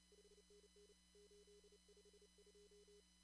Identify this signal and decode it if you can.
Morse code, heard as a keyed tone, about 32 wpm, 410 Hz. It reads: 5NN 852
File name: CW / kt5j wpx cw-op-k5pi